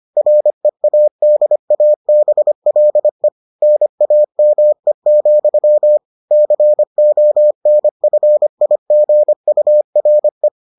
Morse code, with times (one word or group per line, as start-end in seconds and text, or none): 0.2-3.3 READABLE
3.6-6.0 NAME,
6.3-10.5 CONFIGURE